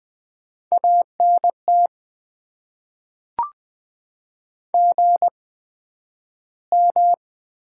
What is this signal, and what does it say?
Morse code, heard as a keyed tone, about 20 wpm, spaced as for 11 wpm, 700 Hz. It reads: ANT GM
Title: Top 90 QSO Elements - 8x Character Spacing - 20wpm